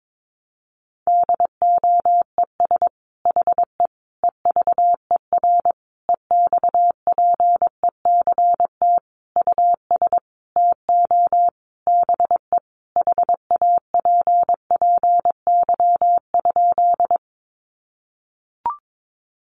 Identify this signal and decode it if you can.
Morse code, heard as a keyed tone, about 22 wpm, 700 Hz. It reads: DOES HE EVER EXPECT US TO BE HAPPY?
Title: Sentences from Top 1100 Words 22wpm